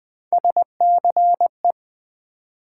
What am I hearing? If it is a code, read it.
Morse code, heard as a keyed tone, about 20 wpm, 700 Hz. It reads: SCE
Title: Build A Word - 1 Letter At A Time - 5-Letter Words - 20wpm